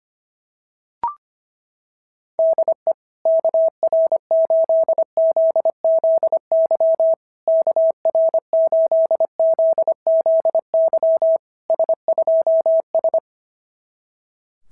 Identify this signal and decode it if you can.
Morse code, heard as a keyed tone, about 22 wpm, 650 Hz. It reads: DE KR8ZZY KR8ZZY S2S
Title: SOTA QSO Practice 25wpm